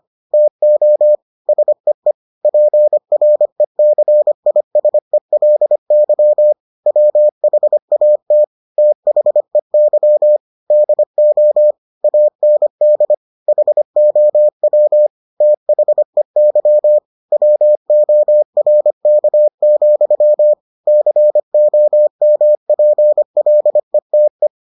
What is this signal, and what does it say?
Morse code, heard as a keyed tone, about 25 wpm, 600 Hz. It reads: TO SEE PRECISELY WHAT THEY DO AND HOW THEY WORK, COMPLETE